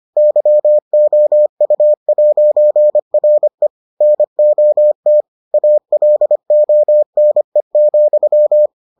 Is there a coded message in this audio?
Yes: Morse code from 0.2 to 8.7 s, heard as a keyed tone, about 25 wpm, 600 Hz. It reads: YOU'RE NOT ALONE,